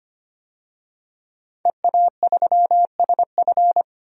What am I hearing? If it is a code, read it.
Morse code, heard as a keyed tone, about 25 wpm, 700 Hz. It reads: EA3SF
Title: DXCC-100 Call Signs - 5 Characters - 25wpm